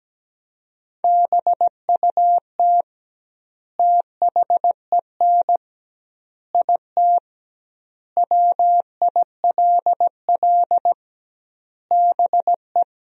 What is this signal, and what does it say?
Morse code, heard as a keyed tone, about 17 wpm, 700 Hz. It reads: BUT THEN IT WILL BE
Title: 2x Word-Spacing - Sentences from Top 300 Words - 17wpm